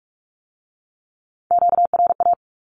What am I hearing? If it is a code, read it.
Morse code, heard as a keyed tone, about 46 wpm, 700 Hz. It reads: QRA
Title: Q-Codes with Spoken Meaning - 45wpm